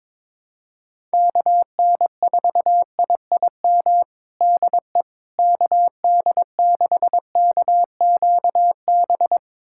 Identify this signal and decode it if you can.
Morse code, heard as a keyed tone, about 21 wpm, 700 Hz. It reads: KN4IIM DE KD6KQB
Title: USA Callsign DE Callsign 22wpm